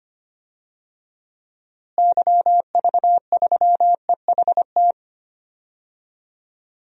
Morse code, heard as a keyed tone, about 25 wpm, 700 Hz. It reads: YV3EHT
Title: DXCC-100 Call Signs - 6 Characters - Repeated 3x - 25wpm